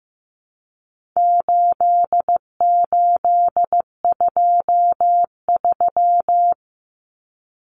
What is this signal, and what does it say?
Morse code, heard as a keyed tone, about 15 wpm, 700 Hz. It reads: 8823
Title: Numbers - 4-Digit 15wpm